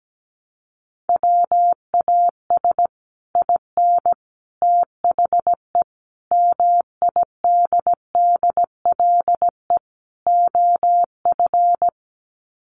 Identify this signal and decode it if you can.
Morse code, heard as a keyed tone, about 17 wpm, 700 Hz. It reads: WAS IN THE MIDDLE OF
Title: Sets of 5 Words Encore 17wpm